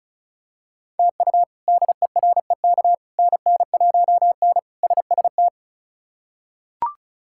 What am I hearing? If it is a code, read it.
Morse code, heard as a keyed tone, about 34 wpm, 700 Hz. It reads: TU DEREK NN1N SST
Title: K1USN SST QSO Practice - 35wpm